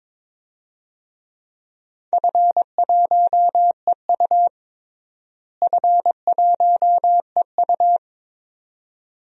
Morse code, heard as a keyed tone, about 22 wpm, 700 Hz. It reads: F1EU F1EU
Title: DXCC-100 Call Signs - 4 Characters - Repeated 2x - 22wpm